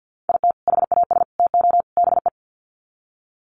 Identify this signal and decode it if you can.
Morse code, heard as a keyed tone, about 50 wpm, 700 Hz. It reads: IT HAS TO BE
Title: Sets of 4 Words 50wpm